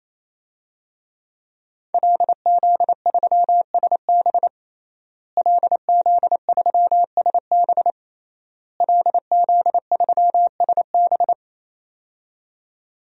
Morse code, heard as a keyed tone, about 27 wpm, 700 Hz. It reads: LZ3SB LZ3SB LZ3SB
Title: DXCC-100 Call Signs - 5 Characters - Repeated 3x - 28wpm